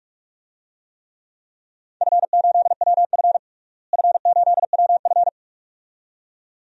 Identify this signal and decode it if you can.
Morse code, heard as a keyed tone, about 46 wpm, 700 Hz. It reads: F8WF F8WF
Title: DXCC-100 Call Signs - 4 Characters - Repeated 2x - 45wpm